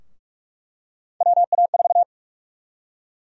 Morse code, heard as a keyed tone, about 46 wpm, 700 Hz. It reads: WA4
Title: USA Call Sign Prefixes - Repeated 2x - 45wpm